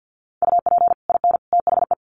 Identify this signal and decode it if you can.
Morse code, heard as a keyed tone, about 50 wpm, 700 Hz. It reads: UP IN THE
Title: Sets of 3 Words 50wpm